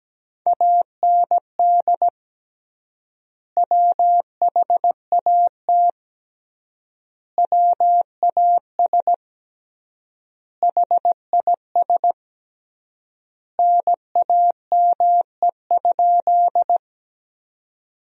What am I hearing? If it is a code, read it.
Morse code, heard as a keyed tone, about 17 wpm, 700 Hz. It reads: AND WHAT WAS HIS NAME?
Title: 3x Word-Spacing - Sentences from Top 200 Words - 17wpm